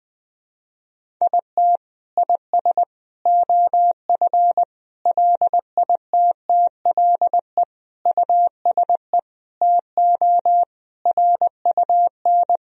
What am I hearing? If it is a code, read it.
Morse code, heard as a keyed tone, about 20 wpm, 700 Hz. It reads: IT IS OF LITTLE USE TO RUN
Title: Sentences from Top 1000 Words - Review Part 1 - 20wpm